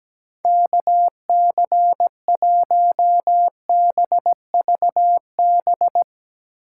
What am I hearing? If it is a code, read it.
Morse code, heard as a keyed tone, about 19 wpm, 700 Hz. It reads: KC1BVB
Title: Speed Racing - USA Call Signs 17wpm